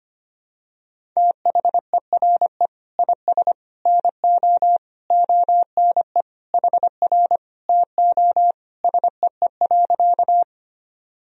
Morse code, heard as a keyed tone, about 25 wpm, 700 Hz. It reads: THERE IS NO ONE HR TO SEE.
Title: Sentences from Top 200 Words and Abbreviations 25wpm